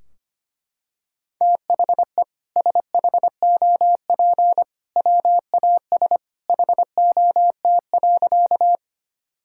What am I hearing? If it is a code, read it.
Morse code, heard as a keyed tone, about 25 wpm, 700 Hz. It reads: THE SHOP WAS HOT.
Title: Sentences from Top 1600 Words 25wpm